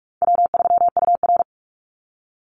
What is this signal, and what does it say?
Morse code, heard as a keyed tone, about 44 wpm, 700 Hz. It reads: W3UR
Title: Call Signs Repeated 3x - 45wpm